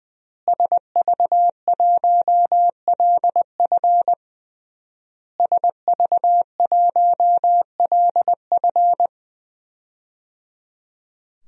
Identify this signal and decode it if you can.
Morse code, heard as a keyed tone, about 20 wpm, 700 Hz. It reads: SV1LF SV1LF
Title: DXCC-100 Call Signs - 5 Characters - Repeated 3x - 20wpm